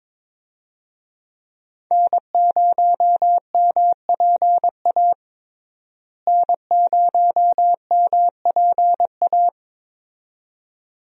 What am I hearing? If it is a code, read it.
Morse code, heard as a keyed tone, about 22 wpm, 700 Hz. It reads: N0MPA N0MPA
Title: USA Call Signs - 5 Characters - Repeated 2x - 22wpm